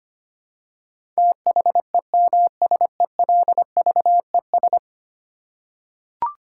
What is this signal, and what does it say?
Morse code, heard as a keyed tone, about 25 wpm, 700 Hz. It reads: THEMSELVES
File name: Ten Letter Words - 25wpm